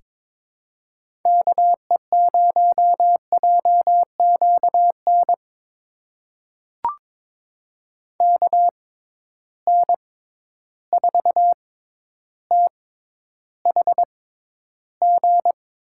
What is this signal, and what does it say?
Morse code, heard as a keyed tone, about 22 wpm, spaced as for 11 wpm, 700 Hz. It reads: KE0JQN KN4THG
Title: USA Call Signs - 6 Characters - 6x Character Spacing - 22wpm